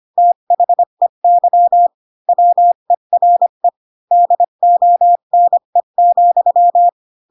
Morse code, heard as a keyed tone, about 25 wpm, 700 Hz. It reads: THEY WERE DONE,
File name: Pinocchio / Pinocchio ch3 20 25 ik1hge